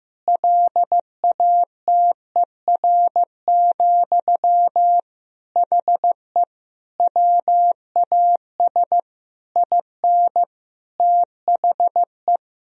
Morse code, heard as a keyed tone, about 15 wpm, 700 Hz. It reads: LATER, HE WAS IN THE
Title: Sentences from Top 1000 Words - Review Part 4 - 15wpm